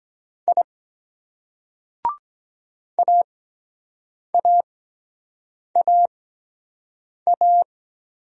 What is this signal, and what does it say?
Morse code, heard as a keyed tone, about 23 wpm, spaced as for 7 wpm, 700 Hz. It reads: I AAAA